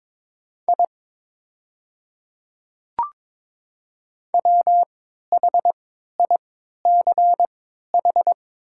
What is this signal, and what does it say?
Morse code, heard as a keyed tone, about 22 wpm, spaced as for 13 wpm, 700 Hz. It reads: I WHICH
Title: 3x Character Spacing - Top 100 Words - 22wpm